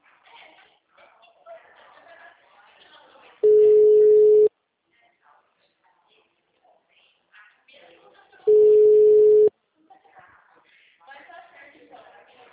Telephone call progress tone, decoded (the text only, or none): ringback tone